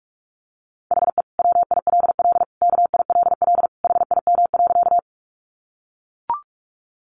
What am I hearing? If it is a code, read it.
Morse code, heard as a keyed tone, about 44 wpm, 700 Hz. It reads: HE WILL KILL HIM.
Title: Sentences from Top 1400 Words 45wpm